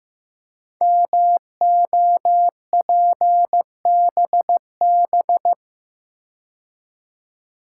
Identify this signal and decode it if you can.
Morse code, heard as a keyed tone, about 15 wpm, 690 Hz. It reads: MOPBB
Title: Code Groups - 5 Letters - 15wpm